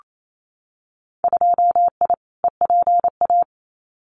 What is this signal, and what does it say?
Morse code, heard as a keyed tone, about 28 wpm, 700 Hz. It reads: 2I EPA
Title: Winter Field Day Exchanges 28wpm